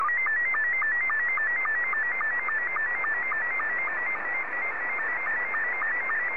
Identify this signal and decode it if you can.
SSTV (slow-scan television), Scottie 2.